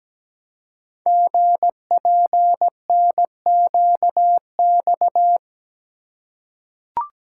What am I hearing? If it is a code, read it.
Morse code, heard as a keyed tone, about 17 wpm, 700 Hz. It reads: GPNQX
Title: Code Groups - 5 Letter:Numbers - 17wpm